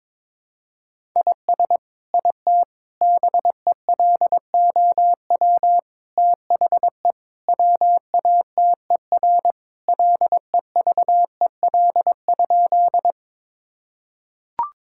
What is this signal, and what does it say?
Morse code, heard as a keyed tone, about 22 wpm, 700 Hz. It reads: IS IT BELOW THE WATER LEVEL?
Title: Sentences from Top 1000 Words - Review Part 5 - 22wpm